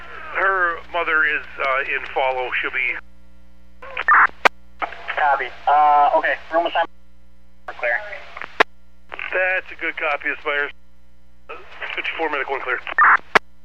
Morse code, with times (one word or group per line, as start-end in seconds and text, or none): none